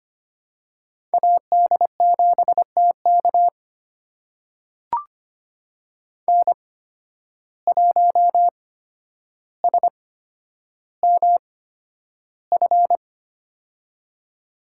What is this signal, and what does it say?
Morse code, heard as a keyed tone, about 25 wpm, spaced as for 11 wpm, 700 Hz. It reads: AD7TK N1SMF
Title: USA Call Signs - 5 Characters - 8x Character Spacing - 25wpm